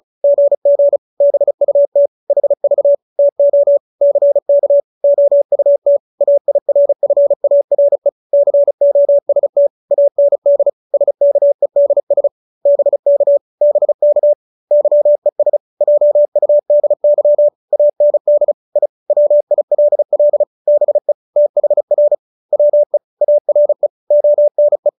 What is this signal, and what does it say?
Morse code, heard as a keyed tone, about 34 wpm, 570 Hz. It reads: GG BUT HV TO CK OUT AIRFARE COST AND SKEDS BK BK YES JUDY AND I WILL BE THR WE ARE ONE
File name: QSO312 35